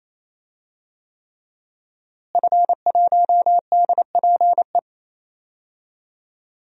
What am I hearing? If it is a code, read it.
Morse code, heard as a keyed tone, about 27 wpm, 700 Hz. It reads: F1DPE
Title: DXCC-100 Call Signs - 5 Characters - Repeated 3x - 28wpm